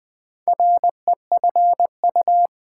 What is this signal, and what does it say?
Morse code, heard as a keyed tone, about 20 wpm, 700 Hz. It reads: REFU
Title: Build A Word - 2 Letters At A Time - 6-Letter Words - 20wpm